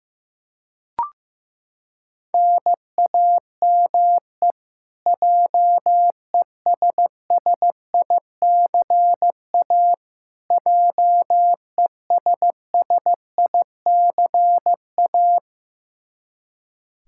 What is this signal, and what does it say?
Morse code, heard as a keyed tone, about 15 wpm, 700 Hz. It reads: NAME JESSICA JESSICA
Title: Practice QSO - Exchange 1 - Beginning - 15wpm